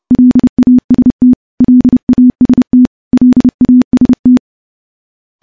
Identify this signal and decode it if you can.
Morse code, heard as a keyed tone, about 30 wpm, 250 Hz. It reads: LAST LAST LAST
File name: morse / morse-training-30WPM